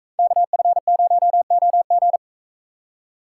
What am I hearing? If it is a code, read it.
Morse code, heard as a keyed tone, about 40 wpm, 700 Hz. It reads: KF0OG